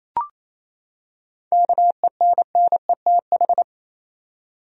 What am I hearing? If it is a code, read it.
Morse code, heard as a keyed tone, about 28 wpm, 700 Hz. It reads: KENNETH
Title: USA Common Names - 28wpm